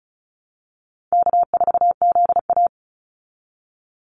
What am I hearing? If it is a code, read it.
Morse code, heard as a keyed tone, about 34 wpm, 700 Hz. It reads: K4ZA
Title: Easy Call Signs Repeated 3x - 35wpm